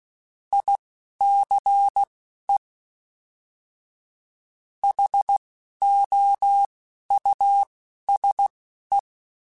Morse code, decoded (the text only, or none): ICE HOUSE